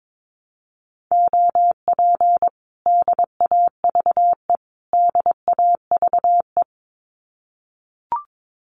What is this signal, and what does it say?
Morse code, heard as a keyed tone, about 22 wpm, 700 Hz. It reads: OP DAVE DAVE
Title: Common QSO Phrases Encore 22wpm